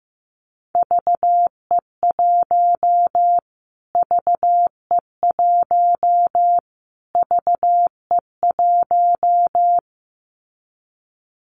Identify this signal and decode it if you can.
Morse code, heard as a keyed tone, about 15 wpm, 700 Hz. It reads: VE1 VE1 VE1
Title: Callsign Prefixes for Top 25 DXCC - Repeated 3x - 15wpm